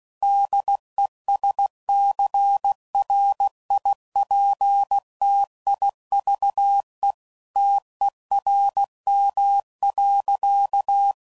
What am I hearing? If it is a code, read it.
Morse code, heard as a keyed tone, about 16 wpm, 800 Hz. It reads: DESCRIPTIVE TERM.
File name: CW Code Groups @15WPM / Technical Subjects / Reactance